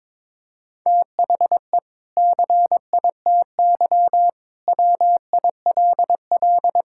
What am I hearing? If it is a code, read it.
Morse code, heard as a keyed tone, about 21 wpm, 700 Hz. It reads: THE CITY WILL
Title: Sentences from Top 1600 Words 22wpm